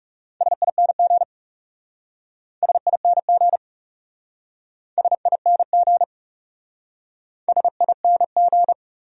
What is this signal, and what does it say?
Morse code, heard as a keyed tone, about 38 wpm, 700 Hz. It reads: SING SING SING SING